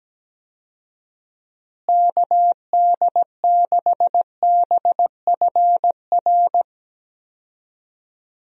Morse code, heard as a keyed tone, about 17 wpm, 700 Hz. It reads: KD6BFR
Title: USA Call Signs - 6 Characters - Repeated 2x - 17wpm